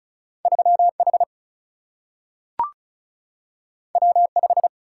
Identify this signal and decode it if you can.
Morse code, heard as a keyed tone, about 36 wpm, 700 Hz. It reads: 3H W5
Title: Single-Letter Single-Number 35wpm